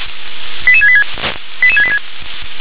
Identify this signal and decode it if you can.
Five-tone selective call (ZVEI), heard as quick runs of five tones, tones of about 70 ms. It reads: 88686 88686